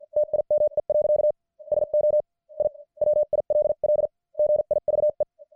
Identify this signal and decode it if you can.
Morse code, heard as a keyed tone, about 50 wpm, spaced as for 44 wpm, 600 Hz. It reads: TIME. SO I WILL GIVE
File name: archived / CWBros